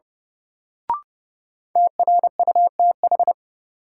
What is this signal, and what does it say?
Morse code, heard as a keyed tone, about 30 wpm, 700 Hz. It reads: TRUTH
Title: Sentences from Top 800 Words - New Word Drill - 30wpm